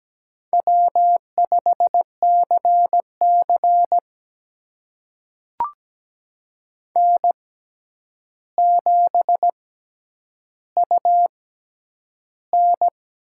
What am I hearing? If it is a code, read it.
Morse code, heard as a keyed tone, about 17 wpm, spaced as for 9 wpm, 700 Hz. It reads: W5CC N7UN
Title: USA Call Signs - 4 Characters - 6x Character Spacing - 17wpm